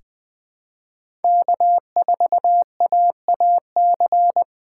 Morse code, heard as a keyed tone, about 20 wpm, 700 Hz. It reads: K4AAC
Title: K1USN SST QSO Practice - 20wpm